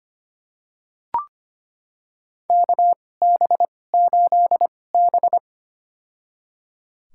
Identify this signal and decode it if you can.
Morse code, heard as a keyed tone, about 25 wpm, spaced as for 18 wpm, 700 Hz. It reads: KB8B
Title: USA Call Signs - 4 Characters - 2x Character Spacing - 25wpm